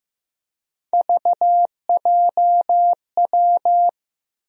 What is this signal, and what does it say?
Morse code, heard as a keyed tone, about 15 wpm, 700 Hz. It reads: VJW
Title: USA Call Sign Postfixes - 3 Letters 15wpm